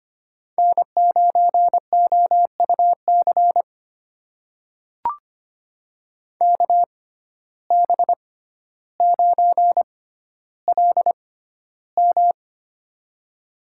Morse code, heard as a keyed tone, about 25 wpm, spaced as for 13 wpm, 700 Hz. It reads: N9OUC KB9LM
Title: USA Call Signs - 5 Characters - 6x Character Spacing - 25wpm